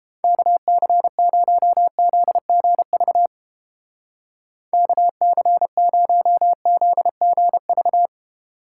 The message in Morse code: KC0ZGV KC0ZGV

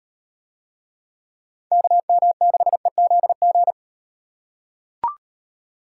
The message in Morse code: KM6EZG

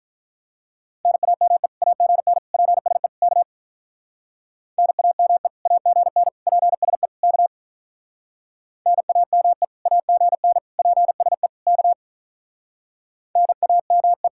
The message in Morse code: NAME AGN PSE K NAME AGN PSE K NAME AGN PSE K NAME

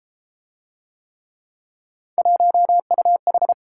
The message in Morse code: 1UH

